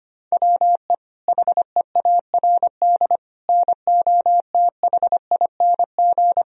Morse code, heard as a keyed tone, about 25 wpm, 700 Hz. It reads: WE HEARD NOTHING
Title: Sentences from Top 1000 Words - Review Part 3 - 25wpm